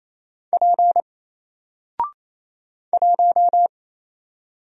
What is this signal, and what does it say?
Morse code, heard as a keyed tone, about 29 wpm, 700 Hz. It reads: P 1
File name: Single-Letter-Number-28wpm